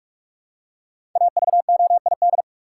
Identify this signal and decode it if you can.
Morse code, heard as a keyed tone, about 42 wpm, 700 Hz. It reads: AVOID
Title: Five Letter Words - 45wpm